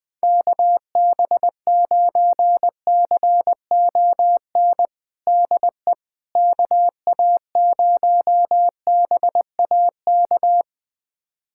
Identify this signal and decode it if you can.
Morse code, heard as a keyed tone, about 20 wpm, 700 Hz. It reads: KB9CON DE KA0BAK
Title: USA Callsign DE Callsign 20wpm